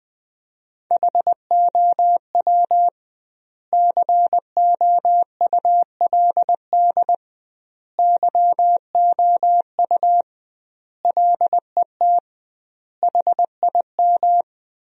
Morse code, heard as a keyed tone, about 20 wpm, 700 Hz. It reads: HOW COULD YOU LET HIM
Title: 2x Word-Spacing - Sentences from Top 300 Words - 20wpm